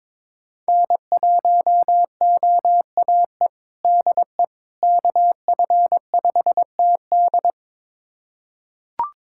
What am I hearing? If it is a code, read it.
Morse code, heard as a keyed tone, about 22 wpm, 700 Hz. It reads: N1OAE DE KF5TD
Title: USA Callsign DE Callsign - 5 Characters - 22wpm